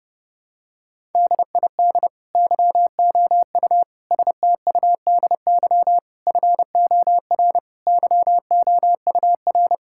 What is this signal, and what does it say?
Morse code, heard as a keyed tone, about 30 wpm, 700 Hz. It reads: DID YOU STUDY FOR YOUR
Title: Sentences from Top 1000 Words - Review Part 6 - 30wpm